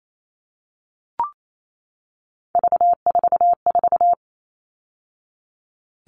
Morse code, heard as a keyed tone, about 28 wpm, 700 Hz. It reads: V44
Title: All Cut Numbers - 3-Digit 28wpm